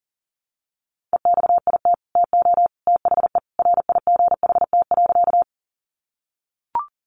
Morse code, heard as a keyed tone, about 40 wpm, 700 Hz. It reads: EXIT TO THE RIGHT.
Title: Sentences from Top 1100 Words 40wpm